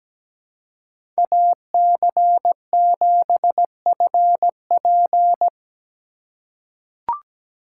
Morse code, heard as a keyed tone, about 17 wpm, 700 Hz. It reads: AC7FP
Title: USA Call Signs - 5 Characters - 4x Character Spacing - 17wpm